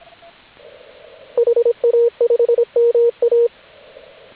Morse code, heard as a keyed tone, about 27 wpm, 470 Hz. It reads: HA5MA